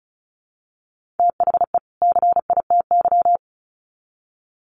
Morse code, heard as a keyed tone, about 34 wpm, 700 Hz. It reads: THE CITY
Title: Sets of 2 Words 35wpm